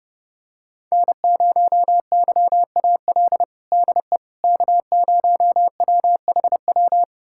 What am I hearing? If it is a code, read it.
Morse code, heard as a keyed tone, about 32 wpm, 700 Hz. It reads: N0YAL DE K0WHW
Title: USA Callsign DE Callsign - 5 Characters - 30wpm